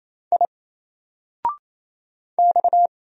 Morse code, heard as a keyed tone, about 27 wpm, 700 Hz. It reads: I X
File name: Single-Letter-Number-28wpm